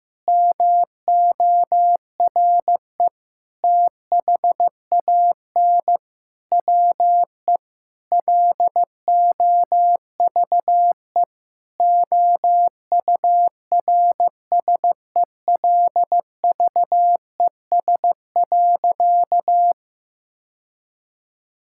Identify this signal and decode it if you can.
Morse code, heard as a keyed tone, about 15 wpm, 700 Hz. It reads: MORE THAN WE LOVE OURSELVES.
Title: Sentences from Top 1600 Words 15wpm